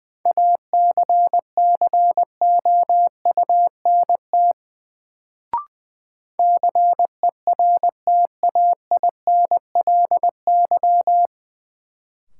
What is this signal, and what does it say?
Morse code, heard as a keyed tone, about 20 wpm, 700 Hz. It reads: ACCOUNT CERTAINLY
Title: Sentences from Top 900 Words - New Word Drill - 20wpm